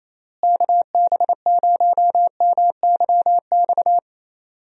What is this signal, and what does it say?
Morse code, heard as a keyed tone, about 28 wpm, 700 Hz. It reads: KB0MYX
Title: USA Call Signs - 6 Characters - Repeated 3x - 28wpm